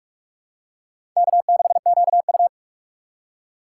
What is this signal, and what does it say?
Morse code, heard as a keyed tone, about 46 wpm, 700 Hz. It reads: K6QU
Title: USA Call Signs - 4 Characters - 3x Character Spacing - 45wpm